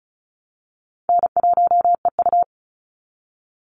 Morse code, heard as a keyed tone, about 36 wpm, 700 Hz. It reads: N1EU
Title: NA SOTA Activator Callsigns 35wpm